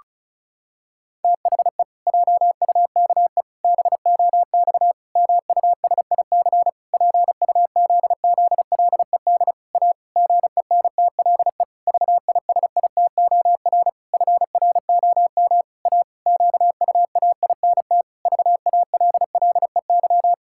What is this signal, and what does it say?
Morse code, heard as a keyed tone, about 34 wpm, 700 Hz. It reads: THE JUKE BOX MUSIC PUZZLED A GENTLE VISITOR FROM A QUAINT VALLEY
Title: Pangram Challenge v2 - 35wpm